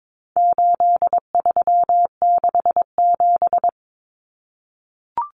Morse code, heard as a keyed tone, about 22 wpm, 700 Hz. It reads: 8367
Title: Common Cut Numbers - 4-Digit 22wpm